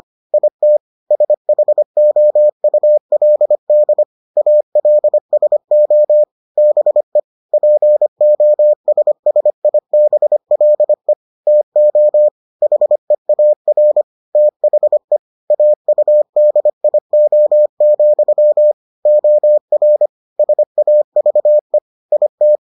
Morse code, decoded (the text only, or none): IT SHOULD ALSO BE POSSIBLE TO HEAR THE AUDIO, OR SAVE IT